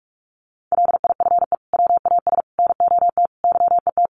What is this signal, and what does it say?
Morse code, heard as a keyed tone, about 44 wpm, 700 Hz. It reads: LIFE WAS NOT YET
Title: Sentences from Top 1200 Words 45wpm